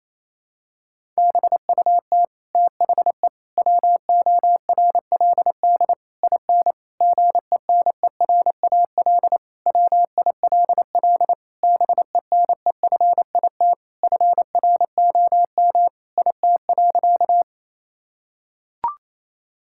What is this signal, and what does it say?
Morse code, heard as a keyed tone, about 28 wpm, 700 Hz. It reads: BUT THE WORLD IN GENERAL WILL BENEFIT FROM IT.
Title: Sentences from Top 1000 Words - Review Part 6 - 28wpm